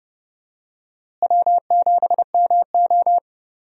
Morse code, heard as a keyed tone, about 32 wpm, 700 Hz. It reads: W7MO
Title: USA Call Signs - 4 Characters - Repeated 3x - 30wpm